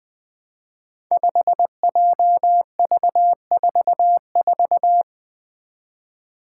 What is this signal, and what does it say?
Morse code, heard as a keyed tone, about 20 wpm, 700 Hz. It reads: 5JV44
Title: Five Challenging Letter-Number Combinations 20wpm